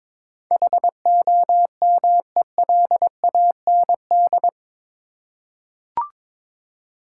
Morse code, heard as a keyed tone, about 22 wpm, 700 Hz. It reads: HOMELAND
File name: practice / 22wpm / Eight Letter Words - 22wpm